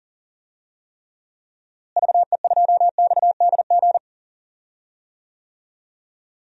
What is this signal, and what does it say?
Morse code, heard as a keyed tone, about 37 wpm, 700 Hz. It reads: VE2XDG
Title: DXCC-100 Call Signs - 6 Characters - Repeated 3x - 40wpm